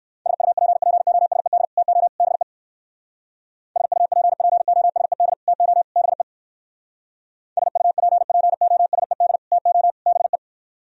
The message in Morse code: SUPPOSED TO BE SUPPOSED TO BE SUPPOSED TO BE

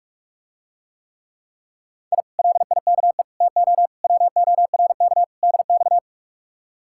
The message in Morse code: I LIKE TO WORK DX

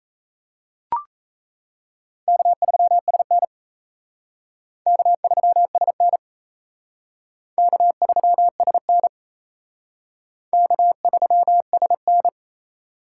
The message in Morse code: K3SN K3SN K3SN K3SN

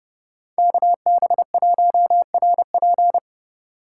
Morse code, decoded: KB1RP

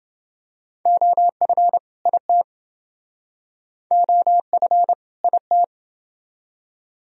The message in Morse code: OF IT OF IT

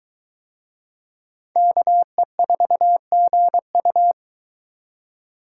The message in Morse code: KE4GU

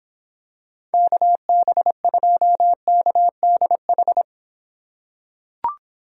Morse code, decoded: KB2KDH

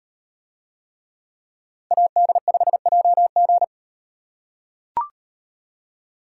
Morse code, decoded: AD5JG

